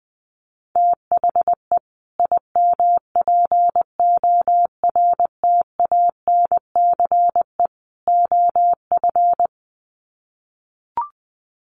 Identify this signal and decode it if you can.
Morse code, heard as a keyed tone, about 19 wpm, 700 Hz. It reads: THE IMPORTANCE OF
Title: Sets of 3 Words 20wpm